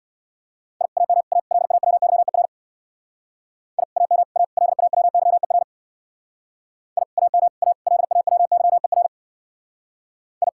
Morse code, heard as a keyed tone, about 64 wpm, 700 Hz. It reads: I AM A LAWYER I AM A LAWYER I AM A LAWYER I